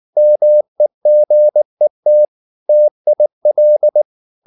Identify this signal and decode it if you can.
Morse code, heard as a keyed tone, about 19 wpm, 600 Hz. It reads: MEGET TIL
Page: Ferie 19 wpm